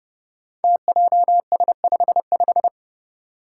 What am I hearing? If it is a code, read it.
Morse code, heard as a keyed tone, about 30 wpm, 700 Hz. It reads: TJS55
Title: Five Challenging Letter-Number Combinations 30wpm